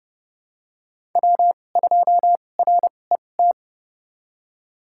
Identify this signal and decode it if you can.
Morse code, heard as a keyed tone, about 30 wpm, spaced as for 22 wpm, 700 Hz. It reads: W2RET
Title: USA Call Signs - 5 Characters - 2x Character Spacing - 30wpm